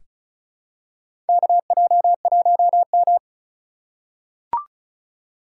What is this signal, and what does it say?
Morse code, heard as a keyed tone, about 34 wpm, 700 Hz. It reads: KJ1M
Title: USA Call Signs - 4 Characters - 4x Character Spacing - 35wpm